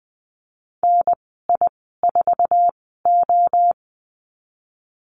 Morse code, heard as a keyed tone, about 20 wpm, spaced as for 14 wpm, 700 Hz. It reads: NI4O
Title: 048 Lesson 9g - Number 4 - Call Signs 20wpm